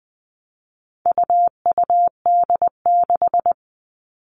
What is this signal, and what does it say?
Morse code, heard as a keyed tone, about 20 wpm, 700 Hz. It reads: UUD6
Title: All Cut Numbers - 4-Digit 20wpm